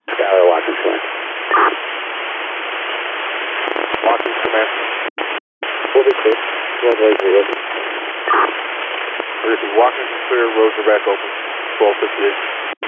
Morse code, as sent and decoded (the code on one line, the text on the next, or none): none